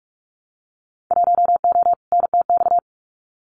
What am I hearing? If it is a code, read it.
Morse code, heard as a keyed tone, about 44 wpm, 700 Hz. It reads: 1O NTX
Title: Winter Field Day Exchanges 45wpm